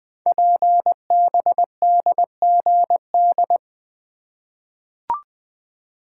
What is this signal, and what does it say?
Morse code, heard as a keyed tone, about 20 wpm, 700 Hz. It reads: PBDGD
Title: Code Groups - 5 Letters - 20wpm